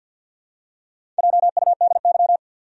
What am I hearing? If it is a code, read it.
Morse code, heard as a keyed tone, about 53 wpm, 700 Hz. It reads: JUDY
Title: USA Common Names - Max 4 Letters - 50wpm